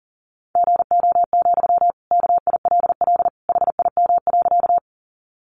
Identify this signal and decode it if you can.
Morse code, heard as a keyed tone, about 40 wpm, 700 Hz. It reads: GO, KILL HIM.
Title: Sentences from Top 1400 Words 40wpm